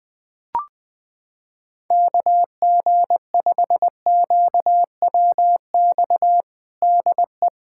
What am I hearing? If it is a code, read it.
Morse code, heard as a keyed tone, about 20 wpm, 700 Hz. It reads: KG5QWX DE
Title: USA Callsign DE Callsign - 6 Characters - 20wpm